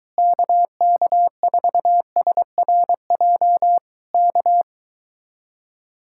Morse code, heard as a keyed tone, about 23 wpm, 700 Hz. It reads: KK4SRJ K